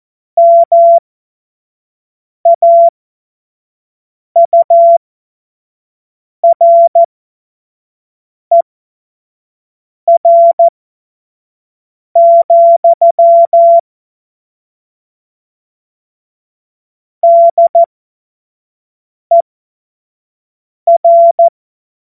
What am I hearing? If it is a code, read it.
Morse code, heard as a keyed tone, about 14 wpm, spaced as for 5 wpm, 680 Hz. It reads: MAURER, DER
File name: Morsekurs / Witze / Witz 24.Oktober 2023 25 F70